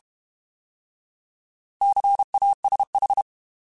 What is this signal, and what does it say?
Morse code, heard as a keyed tone, about 32 wpm, 800 Hz. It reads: CASH
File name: Alphabetical / Alphabetical mp3s / Cs @30